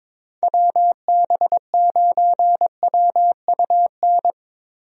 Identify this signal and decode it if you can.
Morse code, heard as a keyed tone, about 21 wpm, 700 Hz. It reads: WB9WUN
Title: USA Call Signs - 6 Characters - Repeated 2x - 22wpm